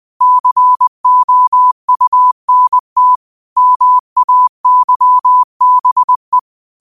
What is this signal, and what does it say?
Morse code, heard as a keyed tone, about 20 wpm, 1000 Hz. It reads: COUNT MAYBE
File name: qotdfiles / 20wpm